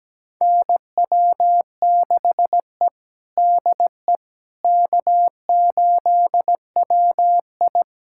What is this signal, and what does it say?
Morse code, heard as a keyed tone, about 17 wpm, 700 Hz. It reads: NW6E DE K8WI
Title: USA Callsign DE Callsign - 4 Characters - 17wpm